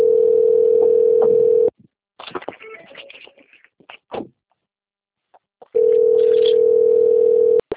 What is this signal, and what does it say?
Ringback tone (North America), 440 + 480 Hz.